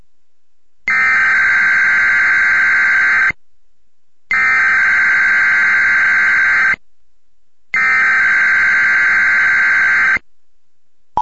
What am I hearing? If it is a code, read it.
Emergency Alert System (SAME) tones (header bursts), heard as harsh buzzing bursts. It is heard from 0.9 to 10.2 s.